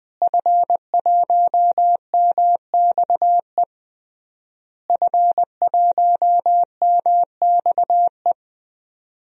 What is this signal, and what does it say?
Morse code, heard as a keyed tone, about 20 wpm, 700 Hz. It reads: F1MXE F1MXE
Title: DXCC-100 Call Signs - 5 Characters - Repeated 3x - 20wpm